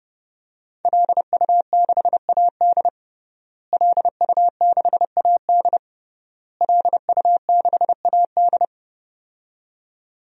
Morse code, heard as a keyed tone, about 30 wpm, 700 Hz. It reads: LU6AD LU6AD LU6AD
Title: DXCC-100 Call Signs - 5 Characters - Repeated 3x - 30wpm